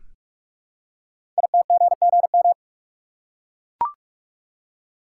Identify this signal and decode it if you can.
Morse code, heard as a keyed tone, about 44 wpm, 700 Hz. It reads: ITGGM
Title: Code Groups - 5 Letters - 45wpm